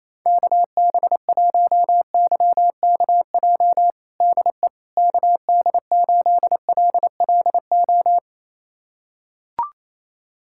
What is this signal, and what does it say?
Morse code, heard as a keyed tone, about 28 wpm, 700 Hz. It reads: KB1YKJ DE KD8LLO
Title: USA Callsign DE Callsign - 6 Characters - 28wpm